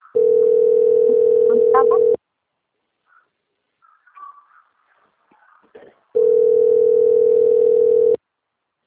Ringback tone (North America), 440 + 480 Hz.
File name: out- - - - - .8659